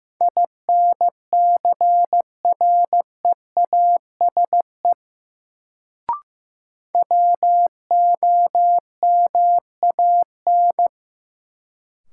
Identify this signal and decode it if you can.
Morse code, heard as a keyed tone, about 15 wpm, 700 Hz. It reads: INCREASE WOMAN
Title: Sets of 5 Words - New Word Drill - 15wpm